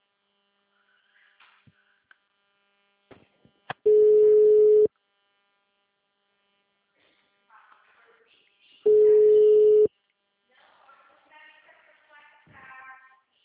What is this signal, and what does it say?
Ringback tone (Europe, 425 Hz), with speech.